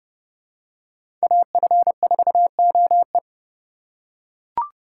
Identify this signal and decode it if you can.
Morse code, heard as a keyed tone, about 32 wpm, 700 Hz. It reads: AF4OE
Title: USA Call Signs - 5 Characters - 2x Character Spacing - 30wpm